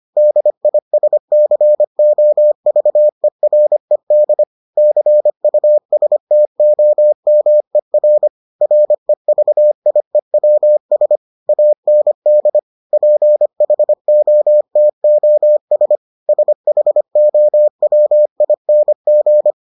Morse code, heard as a keyed tone, about 25 wpm, 600 Hz. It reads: DISCOVERED CUSTOMER REVIEWS AND PHOTOS SHOWING